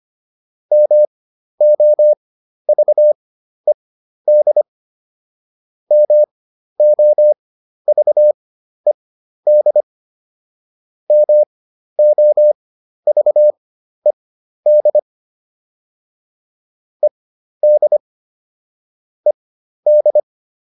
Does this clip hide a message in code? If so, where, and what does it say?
Yes: Morse code from 0.7 to 20.2 s, heard as a keyed tone, about 25 wpm, spaced as for 12 wpm, 600 Hz. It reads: MOVED MOVED MOVED ED ED